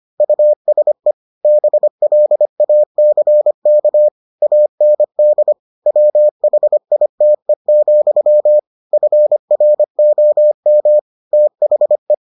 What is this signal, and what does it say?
Morse code, heard as a keyed tone, about 25 wpm, 600 Hz. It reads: USE BLACK AND WHITE, FROM THE